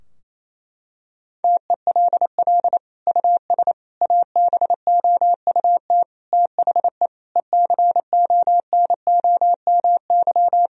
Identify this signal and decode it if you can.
Morse code, heard as a keyed tone, about 28 wpm, 700 Hz. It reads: TELL US ABOUT THE ECONOMY
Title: Sentences from Top 1000 Words - Review Part 3 - 28wpm